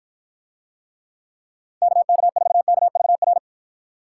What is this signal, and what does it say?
Morse code, heard as a keyed tone, about 53 wpm, 700 Hz. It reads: KK4KVR